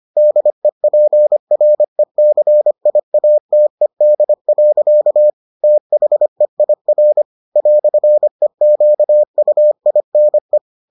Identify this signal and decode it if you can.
Morse code, heard as a keyed tone, about 25 wpm, 600 Hz. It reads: DEPRECIATED. THEIR "EQUINE